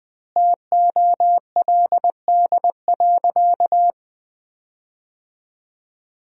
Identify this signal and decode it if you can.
Morse code, heard as a keyed tone, about 20 wpm, 700 Hz. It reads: TOLD.
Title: 4x Word-Spacing - Sentences from Top 400 Words - 20wpm